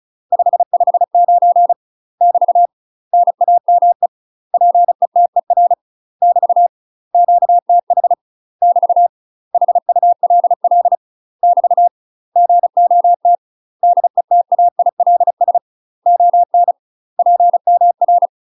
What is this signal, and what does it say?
Morse code, heard as a keyed tone, about 36 wpm, 700 Hz. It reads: 559 <BT> NAME PETER <BT> QTH <BT> HULL <BT> GOT DETAILS ON PMR
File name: UK MASTER QSOs TRAINING / 35WPM / UK MASTER QSOs 0028